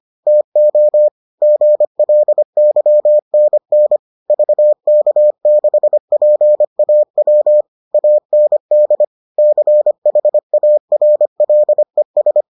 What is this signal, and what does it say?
Morse code, heard as a keyed tone, about 25 wpm, 600 Hz. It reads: TO GLYNN VK6PAW AND CHARLES